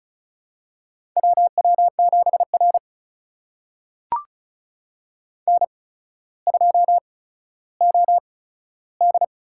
Morse code, heard as a keyed tone, about 36 wpm, spaced as for 15 wpm, 700 Hz. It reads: WW7R N2OD
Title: USA Call Signs - 4 Characters - 8x Character Spacing - 35wpm